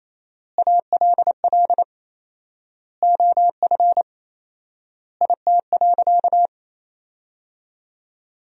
Morse code, heard as a keyed tone, about 28 wpm, 700 Hz. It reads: ALL OF IT.
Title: 4x Word-Spacing - Sentences from Top 500 Words - 28wpm